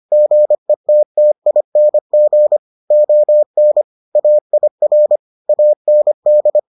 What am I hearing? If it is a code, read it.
Morse code, heard as a keyed tone, about 25 wpm, 600 Hz. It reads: GETTING ON AIR AND